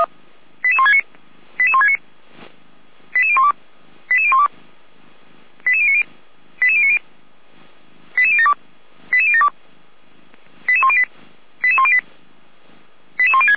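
Five-tone selective call (ZVEI), heard as quick runs of five tones, tones of about 70 ms. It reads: 88169 88169 88012 88012 88090 88090 88072 88072 88108 88108 88106
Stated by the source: ZVEI five-tone call 88352 received by SDR.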